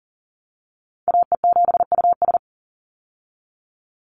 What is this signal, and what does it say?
Morse code, heard as a keyed tone, about 40 wpm, 700 Hz. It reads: AE7US
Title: Call Signs Repeated 3x - 40wpm